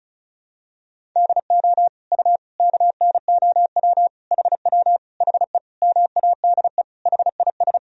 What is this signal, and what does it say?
Morse code, heard as a keyed tone, about 36 wpm, 700 Hz. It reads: DO U KNOW HW HE MADE HIS
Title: Sentences from Top 300 Words and Abbreviations 35wpm